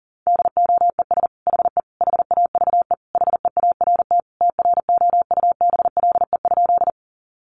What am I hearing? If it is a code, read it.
Morse code, heard as a keyed tone, about 40 wpm, 700 Hz. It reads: DOES HE HAVE HEART TROUBLE?
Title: Sentences from Top 1200 Words 40wpm